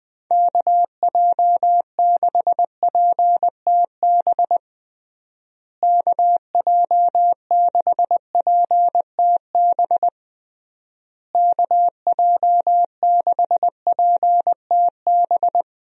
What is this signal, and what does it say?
Morse code, heard as a keyed tone, about 20 wpm, 700 Hz. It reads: KJ6PTB KJ6PTB KJ6PTB
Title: USA Call Signs - 6 Characters - Repeated 3x - 20wpm